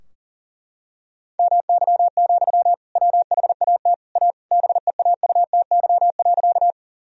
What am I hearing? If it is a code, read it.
Morse code, heard as a keyed tone, about 40 wpm, 700 Hz. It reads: MY, WHAT A BEAUTY.
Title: Sentences from Top 1000 Words - Review Part 2 - 40wpm